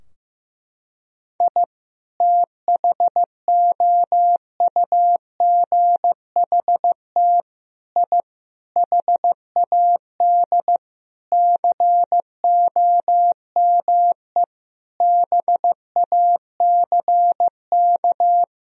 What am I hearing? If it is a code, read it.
Morse code, heard as a keyed tone, about 15 wpm, 700 Hz. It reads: I THOUGHT I HAD COME BACK
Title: Sentences from Top 1000 Words - Review Part 4 - 15wpm